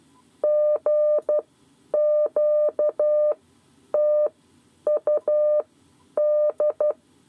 Morse code, heard as a keyed tone, about 12 wpm, spaced as for 9 wpm, 600 Hz. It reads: GQTUD
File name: 7 Lezione cw 2